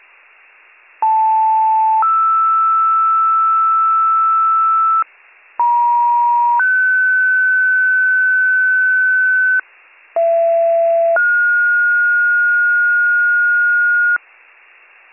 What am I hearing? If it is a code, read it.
Two-tone page, heard as a one-second tone, then a three-second tone, about 880 then 1320 Hz, about 950 then 1500 Hz, about 670 then 1430 Hz. It is heard from 1.0 to 14.2 s.